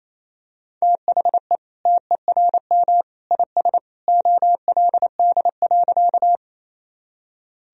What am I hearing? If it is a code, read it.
Morse code, heard as a keyed tone, about 28 wpm, 700 Hz. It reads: THE TERM IS OLD.
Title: Sentences from Top 1000 Words - Review Part 4 - 28wpm